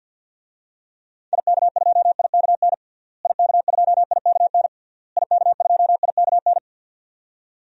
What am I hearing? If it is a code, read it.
Morse code, heard as a keyed tone, about 49 wpm, 700 Hz. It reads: IK2IKN IK2IKN IK2IKN
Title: DXCC-100 Call Signs - 6 Characters - Repeated 3x - 50wpm